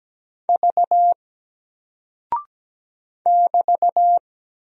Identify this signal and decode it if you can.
Morse code, heard as a keyed tone, about 17 wpm, 700 Hz. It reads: V <BT>
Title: Single Character (Letter:Number:Punctuation:Prosign) Alt 17wpm